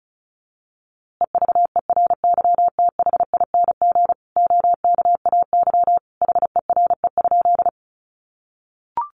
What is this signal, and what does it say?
Morse code, heard as a keyed tone, about 34 wpm, 700 Hz. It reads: EVERYTHING OKAY HERE?
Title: A Sound Of Thunder 35wpm